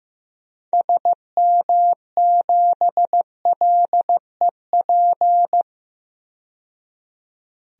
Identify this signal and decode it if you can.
Morse code, heard as a keyed tone, about 15 wpm, 700 Hz. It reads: SM7LEP
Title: DXCC-100 Call Signs - 6 Characters - 15wpm